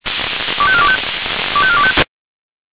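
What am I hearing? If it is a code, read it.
Five-tone selective call (ZVEI), heard as quick runs of five tones, tones of about 70 ms. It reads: 25425 25425
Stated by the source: ZVEI five-tone call received by SDR.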